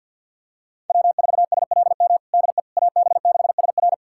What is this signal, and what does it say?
Morse code, heard as a keyed tone, about 47 wpm, 700 Hz. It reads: W4SLM DE AB6SR
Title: USA Callsign DE Callsign - 5 Characters - 50wpm